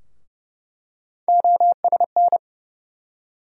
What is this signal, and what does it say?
Morse code, heard as a keyed tone, about 31 wpm, 700 Hz. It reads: OSN
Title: USA Call Sign Postfixes - 3 Letters 30wpm